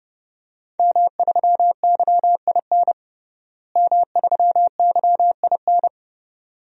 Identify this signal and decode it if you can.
Morse code, heard as a keyed tone, about 30 wpm, 700 Hz. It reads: M3YIN M3YIN
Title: DXCC-100 Call Signs - 5 Characters - Repeated 2x - 30wpm